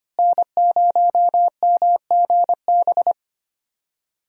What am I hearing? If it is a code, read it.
Morse code, heard as a keyed tone, about 25 wpm, 700 Hz. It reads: N0MGB
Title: USA Call Signs - 5 Characters - Repeated 3x - 25wpm